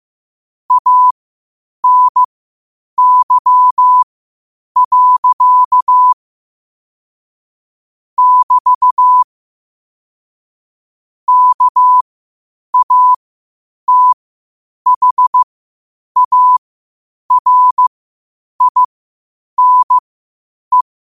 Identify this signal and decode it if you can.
Morse code, heard as a keyed tone, about 15 wpm, spaced as for 8 wpm, 1000 Hz. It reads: ANY. <BT> KATHARINE
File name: 8wpm